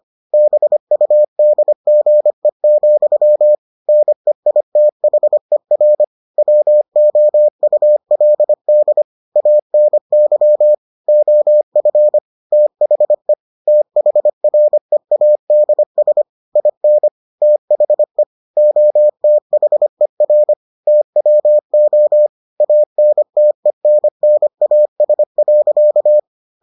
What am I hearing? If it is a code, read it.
Morse code, heard as a keyed tone, about 25 wpm, 600 Hz. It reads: BUDGE, NEITHER WOULD ANY OF THE THREADS IN THE OTHER TWO ANTENNAS.